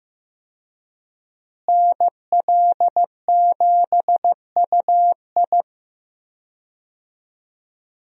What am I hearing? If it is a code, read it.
Morse code, heard as a keyed tone, about 15 wpm, 700 Hz. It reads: NL7UI
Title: USA Call Signs - 5 Characters - Repeated 2x - 15wpm